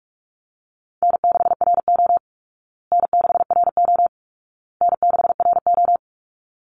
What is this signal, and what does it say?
Morse code, heard as a keyed tone, about 44 wpm, 700 Hz. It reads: N6RO N6RO N6RO
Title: Easy Call Signs Repeated 3x - 45wpm